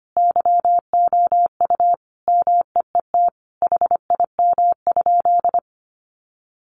Morse code, heard as a keyed tone, about 25 wpm, 700 Hz. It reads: YOU MEET HIM?
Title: Sentences from Top 1100 Words 25wpm